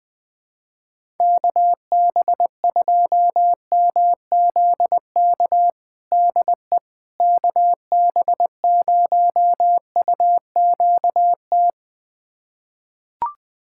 Morse code, heard as a keyed tone, about 20 wpm, 700 Hz. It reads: KB2MZK DE KB0UQT
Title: USA Callsign DE Callsign 20wpm